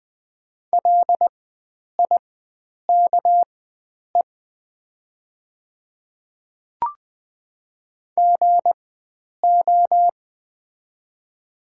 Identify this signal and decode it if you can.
Morse code, heard as a keyed tone, about 20 wpm, spaced as for 9 wpm, 700 Hz. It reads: LIKE GO
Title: 4x Character Spacing - Top 100 Words - 20wpm